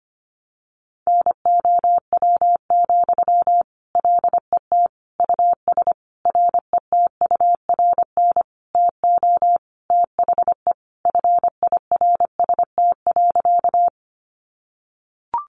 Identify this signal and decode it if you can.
Morse code, heard as a keyed tone, about 25 wpm, 700 Hz. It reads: NOW, LET US RETURN TO THE FIRST.
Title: Sentences from Top 800 Words 25wpm